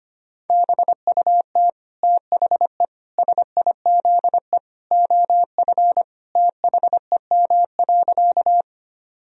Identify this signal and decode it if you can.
Morse code, heard as a keyed tone, about 25 wpm, 700 Hz. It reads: BUT THE SIZE OF THEM.
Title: Sentences from Top 1000 Words - Review Part 2 - 25wpm